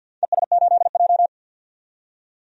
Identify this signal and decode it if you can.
Morse code, heard as a keyed tone, about 45 wpm, 700 Hz. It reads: ES8J
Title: DXCC-100 Call Signs - 4 Characters - 50wpm